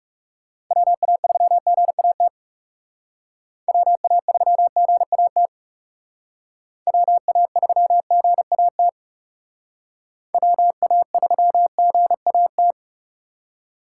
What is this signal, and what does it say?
Morse code, heard as a keyed tone, about 38 wpm, 700 Hz. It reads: WA3GAT WA3GAT WA3GAT WA3GAT